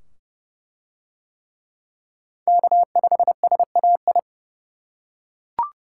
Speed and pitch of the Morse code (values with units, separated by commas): 28 wpm, 700 Hz